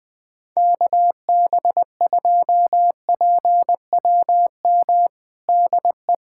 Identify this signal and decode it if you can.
Morse code, heard as a keyed tone, about 21 wpm, 700 Hz. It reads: KB2PWM DE
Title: USA Callsign DE Callsign - 6 Characters - 20wpm